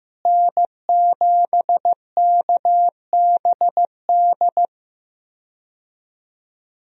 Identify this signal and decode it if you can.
Morse code, heard as a keyed tone, about 15 wpm, 700 Hz. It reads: N7KBD
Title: USA Call Signs - 5 Characters - Repeated 3x - 15wpm